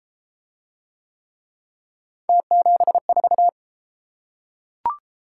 Morse code, heard as a keyed tone, about 29 wpm, 700 Hz. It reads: T74